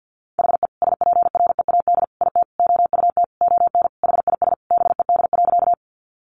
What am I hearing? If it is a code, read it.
Morse code, heard as a keyed tone, about 50 wpm, 700 Hz. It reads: HE SPREAD IT OUT ON HIS BED.
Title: Sentences from Top 1200 Words 50wpm